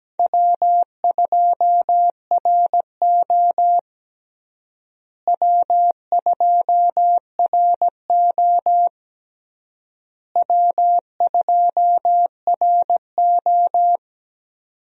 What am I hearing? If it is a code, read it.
Morse code, heard as a keyed tone, about 17 wpm, 700 Hz. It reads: W2RO W2RO W2RO
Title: USA Call Signs - 4 Characters - Repeated 3x - 17wpm